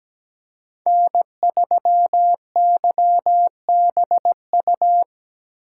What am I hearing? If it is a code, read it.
Morse code, heard as a keyed tone, about 17 wpm, 700 Hz. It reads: N3YBU
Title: USA Call Signs - 5 Characters - 8x Character Spacing - 17wpm